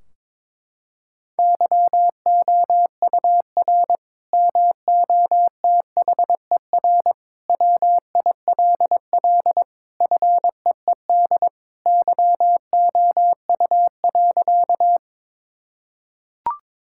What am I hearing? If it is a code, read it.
Morse code, heard as a keyed tone, about 21 wpm, 700 Hz. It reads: YOUR MOTHER WILL FEED YOU.
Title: Sentences from Top 1600 Words 22wpm